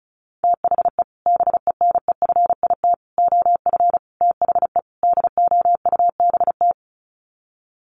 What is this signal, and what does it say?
Morse code, heard as a keyed tone, about 34 wpm, 700 Hz. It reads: THE BENEFIT OF THE DOUBT
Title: Sets of 5 Words Encore 35wpm